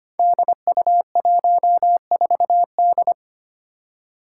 Morse code, heard as a keyed tone, about 25 wpm, 700 Hz. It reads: DU14D
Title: Five Challenging Letter-Number Combinations 25wpm